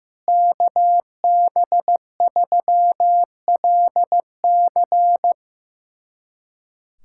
Morse code, heard as a keyed tone, about 15 wpm, 700 Hz. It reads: KB3LC
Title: USA Callsign DE Callsign - 5 Characters - 15wpm